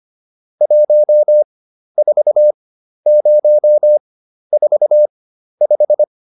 Morse code, heard as a keyed tone, about 25 wpm, spaced as for 12 wpm, 600 Hz. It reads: 14045